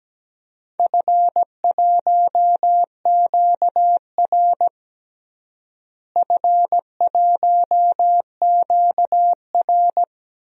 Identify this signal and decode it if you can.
Morse code, heard as a keyed tone, about 17 wpm, 700 Hz. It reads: F1QR F1QR
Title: DXCC-100 Call Signs - 4 Characters - Repeated 2x - 17wpm